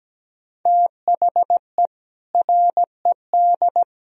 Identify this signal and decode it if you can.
Morse code, heard as a keyed tone, about 17 wpm, 700 Hz. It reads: THE RED
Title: Sentences from Top 1000 Words - Review Part 2 - 17wpm